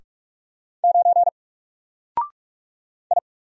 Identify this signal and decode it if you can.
Morse code, heard as a keyed tone, about 47 wpm, 700 Hz. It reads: 9 I
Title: Single Character (Letter:Number:Punctuation:Prosign) Alt 45wpm